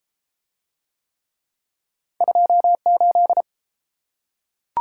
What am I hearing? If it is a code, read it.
Morse code, heard as a keyed tone, about 30 wpm, 700 Hz. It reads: 28